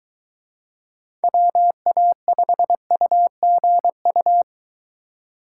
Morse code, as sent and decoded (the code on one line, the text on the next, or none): .-- .- ..... ..- --. ..-
WA5UGU